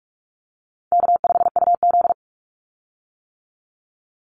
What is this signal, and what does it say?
Morse code, heard as a keyed tone, about 44 wpm, 700 Hz. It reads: K5UZ
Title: Call Signs - 45wpm